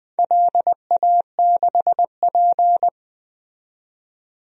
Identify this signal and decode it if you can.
Morse code, heard as a keyed tone, about 20 wpm, 700 Hz. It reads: LA6P
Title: DXCC-100 Call Signs - 4 Characters - 20wpm